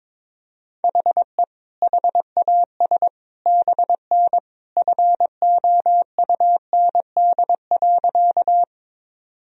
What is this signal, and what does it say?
Morse code, heard as a keyed tone, about 22 wpm, 700 Hz. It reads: HE HAS BN FOUND.
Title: Sentences from Top 300 Words and Abbreviations 22wpm